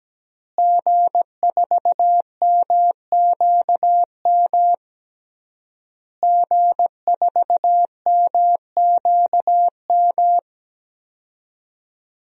Morse code, heard as a keyed tone, about 17 wpm, 700 Hz. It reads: G4MQM G4MQM
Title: DXCC-100 Call Signs - 5 Characters - Repeated 3x - 17wpm